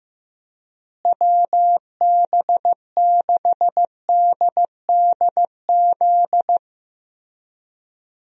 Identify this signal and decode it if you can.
Morse code, heard as a keyed tone, about 16 wpm, 700 Hz. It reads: WB6DDZ
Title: USA Call Signs - 6 Characters - Repeated 3x - 15wpm